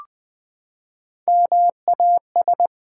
Morse code, heard as a keyed tone, about 20 wpm, 700 Hz. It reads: MAS
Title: Build A Word - 3 Letters At A Time - 8-Letter Words - 20wpm